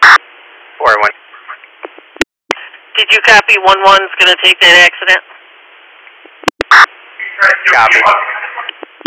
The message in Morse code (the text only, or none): none